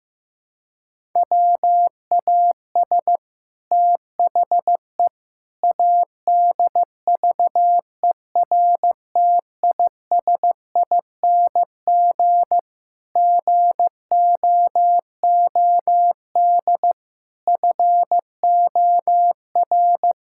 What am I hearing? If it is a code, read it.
Morse code, heard as a keyed tone, about 15 wpm, 700 Hz. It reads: WAS THE ADVERTISING GOOD FOR
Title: Sentences from Top 1000 Words - Review Part 6 - 15wpm